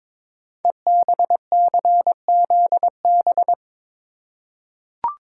Morse code, heard as a keyed tone, about 22 wpm, 700 Hz. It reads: EBCZB
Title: Code Groups - 5 Letters - 22wpm